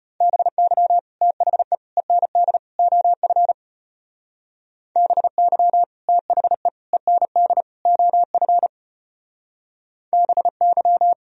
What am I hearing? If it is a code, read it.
Morse code, heard as a keyed tone, about 32 wpm, 700 Hz. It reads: BY THE END OF BY THE END OF BY